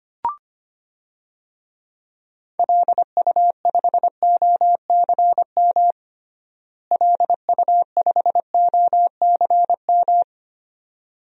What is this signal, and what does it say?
Morse code, heard as a keyed tone, about 25 wpm, 700 Hz. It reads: LU5OCM LU5OCM
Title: DXCC-100 Call Signs - 6 Characters - Repeated 2x - 25wpm